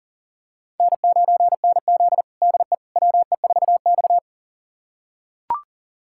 Morse code, heard as a keyed tone, about 40 wpm, 700 Hz. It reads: N9NZ DE WE4X
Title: USA Callsign DE Callsign - 4 Characters - 40wpm